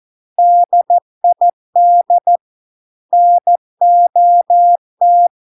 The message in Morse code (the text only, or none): DID NOT